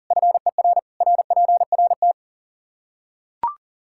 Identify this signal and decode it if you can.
Morse code, heard as a keyed tone, about 40 wpm, 700 Hz. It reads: FER RPRT
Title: Practice QSO - Exchange 2 - Info - 40wpm